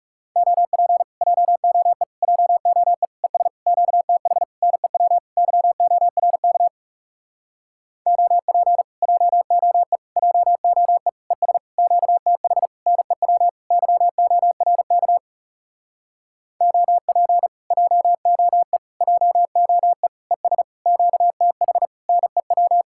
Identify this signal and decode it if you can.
Morse code, heard as a keyed tone, about 40 wpm, 700 Hz. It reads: OP JOE JOE ES QTH NEW YORK OP JOE JOE ES QTH NEW YORK OP JOE JOE ES QTH NEW